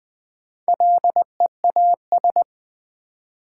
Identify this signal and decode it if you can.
Morse code, heard as a keyed tone, about 20 wpm, 700 Hz. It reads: LEAS
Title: Build A Word - 1 Letter At A Time - 5-Letter Words - 20wpm